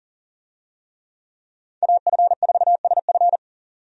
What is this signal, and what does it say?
Morse code, heard as a keyed tone, about 40 wpm, 700 Hz. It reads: AF4SF
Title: USA Call Signs - 5 Characters - 40wpm